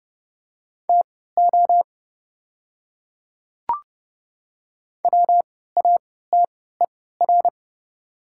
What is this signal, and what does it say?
Morse code, heard as a keyed tone, about 30 wpm, spaced as for 17 wpm, 700 Hz. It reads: TO WATER
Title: 3x Character Spacing - Top 100 Words - 30wpm